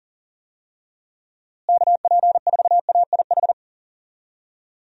Morse code, heard as a keyed tone, about 40 wpm, 700 Hz. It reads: KP4AIH
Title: DXCC-100 Call Signs - 6 Characters - Repeated 2x - 40wpm